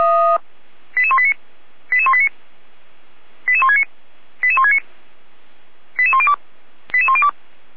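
Five-tone selective call (ZVEI), heard as quick runs of five tones, tones of about 70 ms. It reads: 88189 88189 88169 88169 88102 88102
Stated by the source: ZVEI five-tone selective call received with an SDR.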